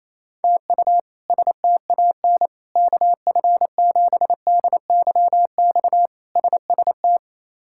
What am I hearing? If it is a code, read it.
Morse code, heard as a keyed tone, about 28 wpm, 700 Hz. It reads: TU STAN KF7DYX SST
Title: K1USN SST QSO Practice - 28wpm